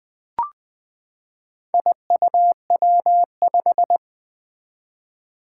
Morse code, heard as a keyed tone, about 20 wpm, 700 Hz. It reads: IUW5
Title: Four Challenging Letter-Number Combinations 20wpm